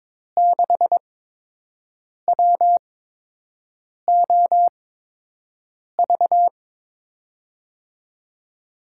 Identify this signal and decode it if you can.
Morse code, heard as a keyed tone, about 22 wpm, spaced as for 6 wpm, 700 Hz. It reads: 6WOV
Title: USA Call Signs - 5 Characters - 8x Character Spacing - 22wpm